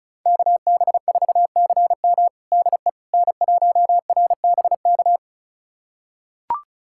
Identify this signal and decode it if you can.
Morse code, heard as a keyed tone, about 36 wpm, 700 Hz. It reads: KB4CM DE N1RBK
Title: USA Callsign DE Callsign - 5 Characters - 35wpm